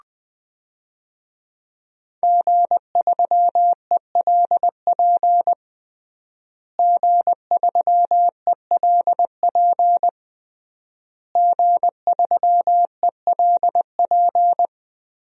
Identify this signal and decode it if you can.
Morse code, heard as a keyed tone, about 20 wpm, 700 Hz. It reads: G3ELP G3ELP G3ELP
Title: DXCC-100 Call Signs - 5 Characters - Repeated 3x - 20wpm